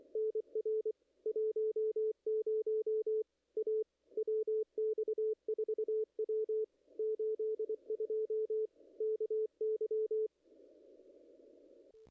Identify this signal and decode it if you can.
Morse code, heard as a keyed tone, about 24 wpm, 440 Hz. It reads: NR 10 A WX4W 82 KY
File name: contests / ss-cw-k5tr-op-n5rz -0100